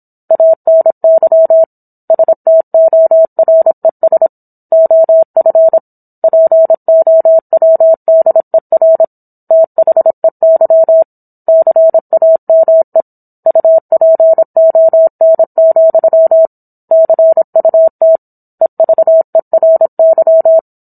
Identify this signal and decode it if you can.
Morse code, heard as a keyed tone, about 27 wpm, 650 Hz. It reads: ANY STORES OF POWDER THEY CAME UPON, CUT EVERY